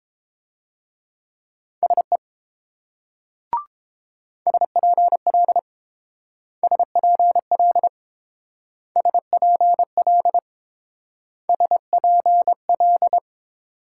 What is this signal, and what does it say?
Morse code, heard as a keyed tone, about 28 wpm, 700 Hz. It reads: SE SPL SPL SPL SPL